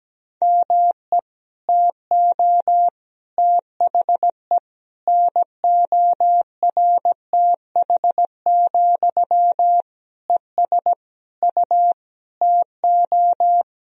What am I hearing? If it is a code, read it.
Morse code, heard as a keyed tone, about 17 wpm, 700 Hz. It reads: ME TO THE NORTH, ES U TO
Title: Sentences from Top 400 Words and Abbreviations 17wpm